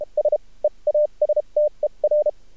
Morse code, heard as a keyed tone, about 32 wpm, spaced as for 27 wpm, 600 Hz. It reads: S EASTER
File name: archived / CRSnet